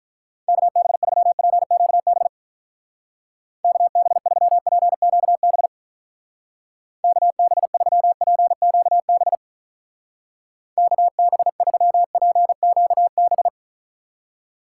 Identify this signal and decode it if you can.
Morse code, heard as a keyed tone, about 44 wpm, 700 Hz. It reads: KB3PQB KB3PQB KB3PQB KB3PQB
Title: Speed Racing - USA 2x3 Call Signs 35wpm